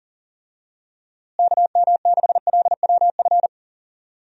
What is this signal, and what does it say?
Morse code, heard as a keyed tone, about 40 wpm, 700 Hz. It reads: KM6LWF
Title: USA Call Signs - 6 Characters - Repeated 2x - 40wpm